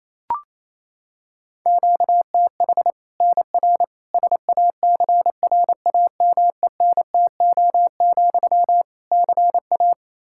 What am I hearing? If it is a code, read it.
Morse code, heard as a keyed tone, about 28 wpm, 700 Hz. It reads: QTH NR SACRAMENTO, CA
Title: Practice QSO - Exchange 1 - Beginning - 28wpm